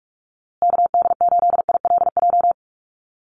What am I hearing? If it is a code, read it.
Morse code, heard as a keyed tone, about 44 wpm, 700 Hz. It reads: KD8ILJ
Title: USA Call Signs Repeated 3x 45wpm